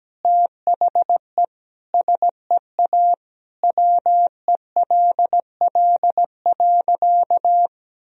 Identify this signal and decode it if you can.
Morse code, heard as a keyed tone, about 17 wpm, 700 Hz. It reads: THE SEA WELL.
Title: Sentences from Top 1000 Words - Review Part 1 - 17wpm